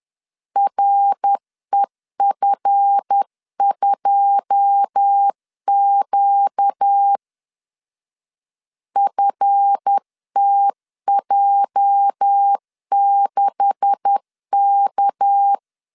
Morse code, heard as a keyed tone, about 10 wpm, 800 Hz. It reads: REF2Q FTJ6K
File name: LEZIONI / 9 286 10wpm 2735